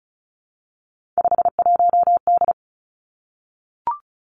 Morse code, heard as a keyed tone, about 36 wpm, 700 Hz. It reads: 51D
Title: Three Challenging Letter-Number Combinations 35wpm